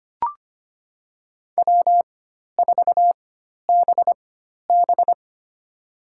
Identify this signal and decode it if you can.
Morse code, heard as a keyed tone, about 25 wpm, spaced as for 12 wpm, 700 Hz. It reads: W4BB
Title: USA Call Signs - 4 Characters - 4x Character Spacing - 25wpm